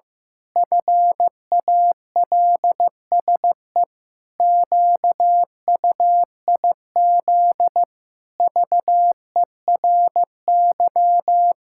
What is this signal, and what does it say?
Morse code, heard as a keyed tone, about 14 wpm, 700 Hz. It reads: FALSE QUIZ VERY
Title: Pangram Challenge v2 - 15wpm